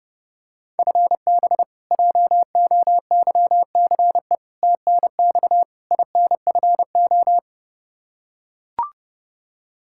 Morse code, heard as a keyed tone, about 30 wpm, 700 Hz. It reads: FB JOYCE TNX INFO
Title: K1USN SST QSO Practice 30wpm